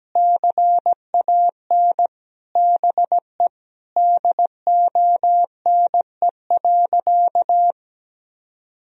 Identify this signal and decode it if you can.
Morse code, heard as a keyed tone, about 17 wpm, 700 Hz. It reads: CAN BE DONE.
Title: Sentences from Top 1000 Words - Review Part 1 - 17wpm